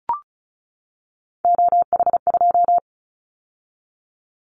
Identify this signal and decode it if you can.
Morse code, heard as a keyed tone, about 36 wpm, 700 Hz. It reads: OH2
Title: Callsign Prefixes for Top 50 DXCC - 35wpm